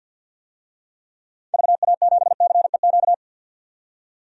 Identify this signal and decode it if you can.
Morse code, heard as a keyed tone, about 49 wpm, 700 Hz. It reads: VA7CEQ
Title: DXCC-100 Call Signs - 6 Characters - 50wpm